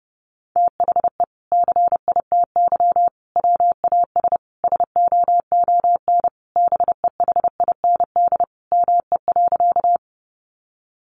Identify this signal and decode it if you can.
Morse code, heard as a keyed tone, about 28 wpm, 700 Hz. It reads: THE CITY WAS SOON BEHIND ME.
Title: Sentences from Top 500 Words 30wpm